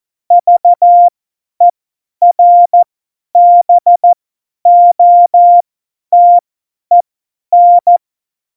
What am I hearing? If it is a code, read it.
Morse code, heard as a keyed tone, about 14 wpm, spaced as for 10 wpm, 700 Hz. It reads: VERBOTEN
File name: Morsekurs / Witze / Witz 09. Oktober 2018, 50 F70